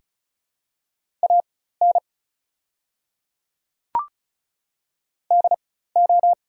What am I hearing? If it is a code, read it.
Morse code, heard as a keyed tone, about 36 wpm, spaced as for 16 wpm, 690 Hz. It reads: AN DO
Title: 4x Character Spacing - Top 200 Words - 35wpm